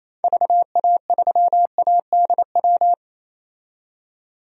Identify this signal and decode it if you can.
Morse code, heard as a keyed tone, about 28 wpm, 700 Hz. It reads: VA3ADW
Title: DXCC-100 Call Signs - 6 Characters - 28wpm